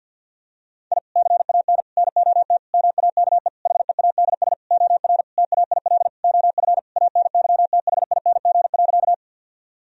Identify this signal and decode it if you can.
Morse code, heard as a keyed tone, about 50 wpm, 700 Hz. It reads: I CAN NOT MAKE HEADS OR TAIL OF ANYTHING.
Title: Sentences from Top 1000 Words - Review Part 3 - 50wpm